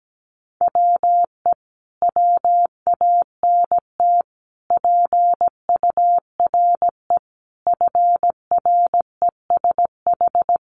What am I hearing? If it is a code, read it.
Morse code, heard as a keyed tone, about 17 wpm, 700 Hz. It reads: WE WANT PURE FRESH
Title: Sentences from Top 1500 Words 17wpm